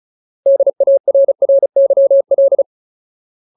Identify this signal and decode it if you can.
Morse code, heard as a keyed tone, about 35 wpm, 550 Hz. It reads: DARRYL
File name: Ultimate Word List / 35WPM Ultimate